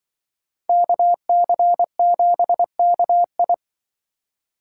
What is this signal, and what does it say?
Morse code, heard as a keyed tone, about 25 wpm, 700 Hz. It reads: KC7KI
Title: Speed Racing - USA Call Signs 20wpm